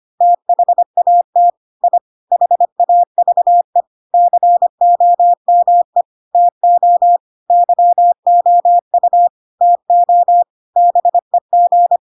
Morse code, heard as a keyed tone, about 25 wpm, 700 Hz. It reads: THAT I HAVE COME TO YOU TO BEG
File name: Pinocchio / Pinocchio ch2 25 25 ik1hge